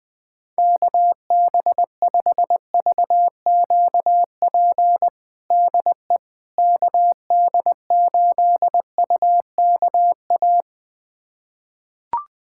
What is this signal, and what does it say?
Morse code, heard as a keyed tone, about 21 wpm, 700 Hz. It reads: KB5VQP DE KD8UKA
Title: USA Callsign DE Callsign - 6 Characters - 20wpm